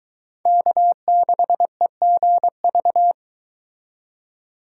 Morse code, heard as a keyed tone, about 22 wpm, 700 Hz. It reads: K6EGV